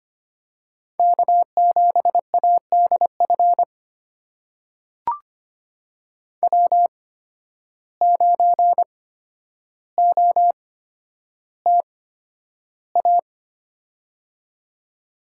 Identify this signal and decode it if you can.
Morse code, heard as a keyed tone, about 25 wpm, spaced as for 11 wpm, 700 Hz. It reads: K7ADF W9OTA
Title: USA Call Signs - 5 Characters - 8x Character Spacing - 25wpm